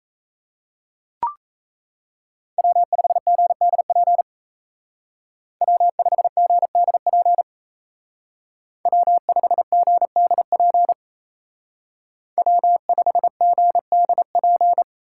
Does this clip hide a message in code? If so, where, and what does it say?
Yes: Morse code from 2.6 to 14.8 s, heard as a keyed tone, about 36 wpm, 700 Hz. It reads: W5GDP W5GDP W5GDP W5GDP